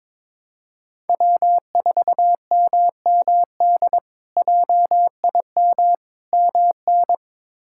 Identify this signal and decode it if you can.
Morse code, heard as a keyed tone, about 22 wpm, 700 Hz. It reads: W4MMD JIM MN
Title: K1USN SST QSO Practice - 22wpm